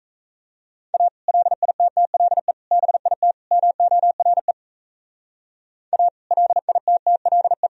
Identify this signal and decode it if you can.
Morse code, heard as a keyed tone, about 41 wpm, 700 Hz. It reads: A LITTLE BIT MORE A LITTLE